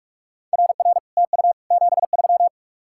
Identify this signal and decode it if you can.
Morse code, heard as a keyed tone, about 49 wpm, 700 Hz. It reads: RR TU 73